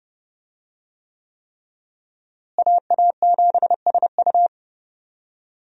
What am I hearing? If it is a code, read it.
Morse code, heard as a keyed tone, about 30 wpm, 700 Hz. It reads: AA7SU
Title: USA Call Signs - 5 Characters - Repeated 3x - 30wpm